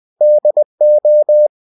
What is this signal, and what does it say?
Morse code, heard as a keyed tone, about 20 wpm, 600 Hz. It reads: DO
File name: cw-books / winnie-the-pooh / winnie-20wpm-page-0024